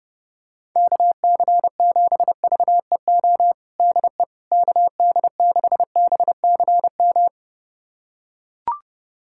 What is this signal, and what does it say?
Morse code, heard as a keyed tone, about 30 wpm, 690 Hz. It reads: KC7VEO DE KD6BCM
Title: USA Callsign DE Callsign - 6 Characters - 30wpm